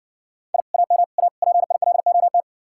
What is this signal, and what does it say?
Morse code, heard as a keyed tone, about 59 wpm, 700 Hz. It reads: I AM A PILOT